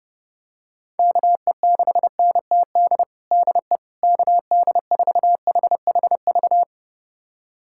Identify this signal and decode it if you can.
Morse code, heard as a keyed tone, about 30 wpm, 700 Hz. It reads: KE6NTD DE KD4HHV
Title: USA Callsign DE Callsign 30wpm